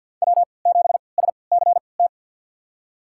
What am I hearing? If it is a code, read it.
Morse code, heard as a keyed tone, about 51 wpm, spaced as for 28 wpm, 700 Hz. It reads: W7SCT
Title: USA Call Signs - 3x Character Spacing - 50wpm